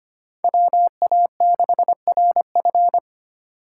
Morse code, heard as a keyed tone, about 25 wpm, 700 Hz. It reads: WA6RF
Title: USA Call Signs - 5 Characters - 4x Character Spacing - 25wpm